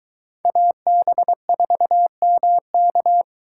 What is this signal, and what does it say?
Morse code, heard as a keyed tone, about 22 wpm, 700 Hz. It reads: AB4MK